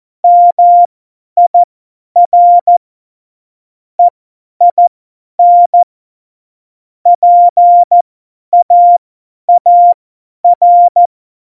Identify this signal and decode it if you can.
Morse code, heard as a keyed tone, about 14 wpm, spaced as for 10 wpm, 700 Hz. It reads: MIR EIN PAAR
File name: Morsekurs / Witze / Witz 19. Januar 2021, 50 F70